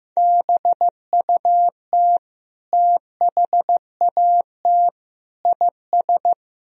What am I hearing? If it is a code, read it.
Morse code, heard as a keyed tone, about 15 wpm, 700 Hz. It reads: BUT THAT IS
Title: Sentences from Top 1000 Words - Review Part 3 - 15wpm